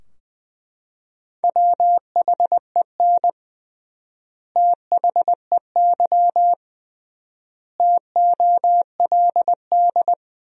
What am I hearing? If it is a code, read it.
Morse code, heard as a keyed tone, about 20 wpm, 700 Hz. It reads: WHEN THEY TOLD
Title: 3x Word-Spacing - Sentences from Top 400 Words - 20wpm